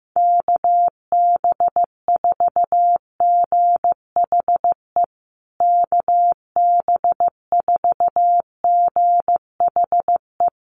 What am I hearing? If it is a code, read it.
Morse code, heard as a keyed tone, about 15 wpm, 700 Hz. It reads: KB4GHE KB4GHE
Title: USA Call Signs Repeated 3x 15wpm